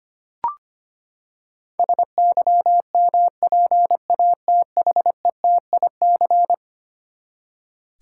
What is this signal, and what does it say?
Morse code, heard as a keyed tone, about 25 wpm, 700 Hz. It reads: SYMPATHETIC
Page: morse-code-ninja-practice-25wpm